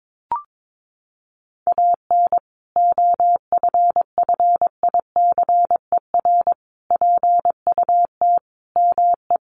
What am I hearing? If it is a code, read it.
Morse code, heard as a keyed tone, about 22 wpm, 700 Hz. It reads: AN OFFICER PUT ME
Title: Sentences from Top 1200 Words 22wpm